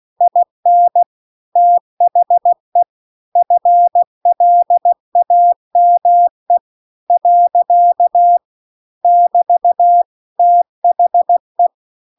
Morse code, heard as a keyed tone, about 16 wpm, 700 Hz. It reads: IN THE FLAME . <BT> THE